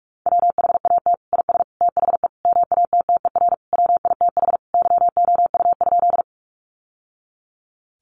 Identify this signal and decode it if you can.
Morse code, heard as a keyed tone, about 44 wpm, 700 Hz. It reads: WHAT IS THE MATTER WITH YOU?
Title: Sentences from Top 700 Words 45wpm